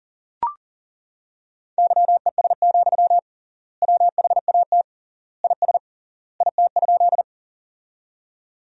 Morse code, heard as a keyed tone, about 40 wpm, 700 Hz. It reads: YES, WHAT IS IT?
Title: 3x Word-Spacing - Sentences from Top 500 Words - 40wpm